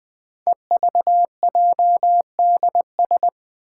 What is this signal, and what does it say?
Morse code, heard as a keyed tone, about 20 wpm, 700 Hz. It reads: EVJDS
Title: Code Groups - 5 Letters - 20wpm